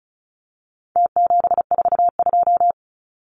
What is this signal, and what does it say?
Morse code, heard as a keyed tone, about 34 wpm, 700 Hz. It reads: T742
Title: Numbers - Contest Style - 4-Digit 35wpm